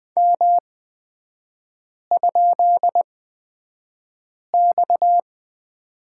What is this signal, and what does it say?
Morse code, heard as a keyed tone, about 20 wpm, spaced as for 5 wpm, 690 Hz. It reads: M?X
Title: 229 Lesson 37e - Letter x - Three Characters 20wpm